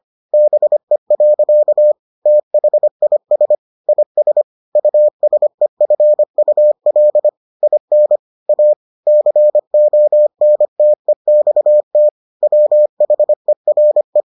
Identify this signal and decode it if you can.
Morse code, heard as a keyed tone, about 25 wpm, 600 Hz. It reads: BE. THIS IS USEFUL IN A CONTEXT WHERE